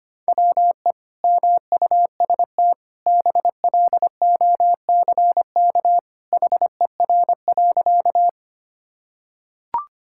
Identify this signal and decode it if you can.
Morse code, heard as a keyed tone, about 25 wpm, 700 Hz. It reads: WE MUST BLOCK HER.
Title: Sentences from Top 1600 Words 25wpm